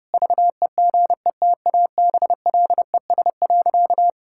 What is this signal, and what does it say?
Morse code, heard as a keyed tone, about 32 wpm, 700 Hz. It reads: VEGETABLES.
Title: Pangram Challenge v2 - 30wpm